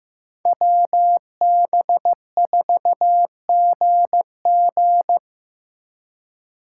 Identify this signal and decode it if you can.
Morse code, heard as a keyed tone, about 18 wpm, 700 Hz. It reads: WB4GG
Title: Speed Racing - USA 2x2 Call Signs 15wpm